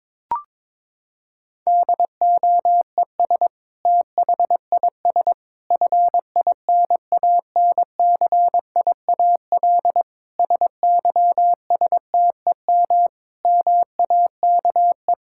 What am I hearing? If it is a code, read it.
Morse code, heard as a keyed tone, about 22 wpm, 700 Hz. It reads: DOES THIS FINANCIAL SYSTEM MAKE
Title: Sentences from Top 1000 Words - Review Part 5 - 22wpm